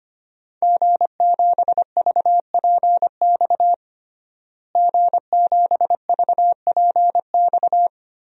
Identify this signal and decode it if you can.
Morse code, heard as a keyed tone, about 25 wpm, 700 Hz. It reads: G7VPX G7VPX
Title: DXCC-100 Call Signs - 5 Characters - Repeated 3x - 25wpm